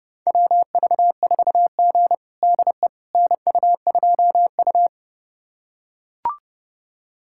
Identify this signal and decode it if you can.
Morse code, heard as a keyed tone, about 30 wpm, 700 Hz. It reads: WV4G DE NU2U
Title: USA Callsign DE Callsign - 4 Characters - 30wpm